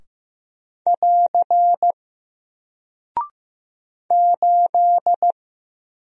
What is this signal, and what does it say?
Morse code, heard as a keyed tone, about 15 wpm, 700 Hz. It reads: <AR> 8
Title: Single Character (Letter:Number:Punctuation:Prosign) Alt 15wpm